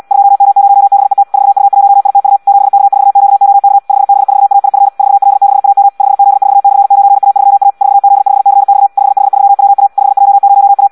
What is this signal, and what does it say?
Morse code, heard as a keyed tone, about 74 wpm, 800 Hz. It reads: 2M0BET 4U1ITU 7K4QOK 5B4AIF 4L6AM 5Z4/9A3A 4Z5CX 5H3WA 5J0A
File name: Audio practica CW / qrq-tx-74 2